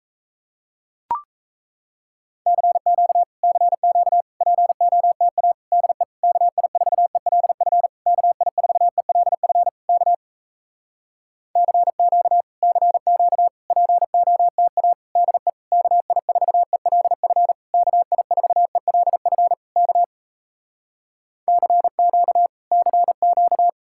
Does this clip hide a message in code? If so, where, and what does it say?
Yes: Morse code from 2.5 to 23.7 s, heard as a keyed tone, about 37 wpm, 700 Hz. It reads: CQ CQ POTA DE KI4ELF KI4ELF K CQ CQ POTA DE KI4ELF KI4ELF K CQ CQ